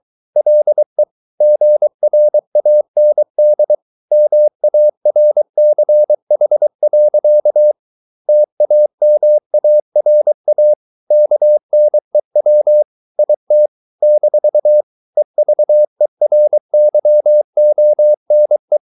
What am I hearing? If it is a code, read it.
Morse code, heard as a keyed tone, about 23 wpm, 600 Hz. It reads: LE GRAND MARCH. TAMARA KNEW IT - EVERYONE